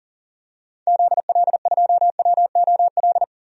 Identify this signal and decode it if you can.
Morse code, heard as a keyed tone, about 40 wpm, 700 Hz. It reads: ZL2WOL
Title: DXCC-100 Call Signs - 6 Characters - 40wpm